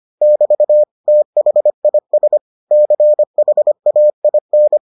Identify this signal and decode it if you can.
Morse code, heard as a keyed tone, about 25 wpm, 600 Hz. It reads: <BT> THIS CHAIN